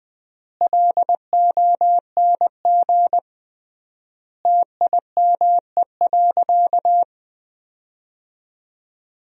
Morse code, heard as a keyed tone, about 20 wpm, 700 Hz. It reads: LONG TIME.
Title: 3x Word-Spacing - Sentences from Top 100 Words - 20wpm